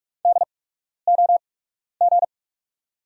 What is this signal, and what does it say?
Morse code, heard as a keyed tone, about 44 wpm, spaced as for 12 wpm, 690 Hz. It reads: DOG
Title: 8x Character Spacing - Top 500 Words - 45wpm